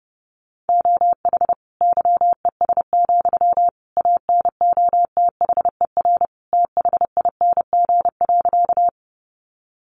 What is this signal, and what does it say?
Morse code, heard as a keyed tone, about 30 wpm, 700 Hz. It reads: OH YES, ANOTHER THING.
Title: Sentences from Top 600 Words 30wpm